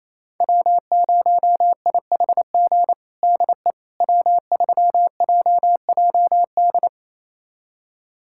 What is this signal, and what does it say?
Morse code, heard as a keyed tone, about 28 wpm, 700 Hz. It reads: W0IHG DE W3JJD
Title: USA Callsign DE Callsign - 5 Characters - 28wpm